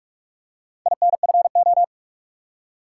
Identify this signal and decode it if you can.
Morse code, heard as a keyed tone, about 44 wpm, 700 Hz. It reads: INFO
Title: Top 180 QSO Elements 45wpm